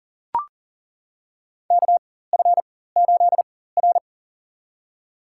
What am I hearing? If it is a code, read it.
Morse code, heard as a keyed tone, about 37 wpm, spaced as for 18 wpm, 700 Hz. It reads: KF8R
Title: USA Call Signs - 4 Characters - 4x Character Spacing - 40wpm